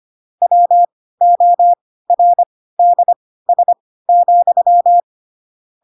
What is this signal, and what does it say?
Morse code, heard as a keyed tone, about 25 wpm, spaced as for 16 wpm, 700 Hz. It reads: WORDS,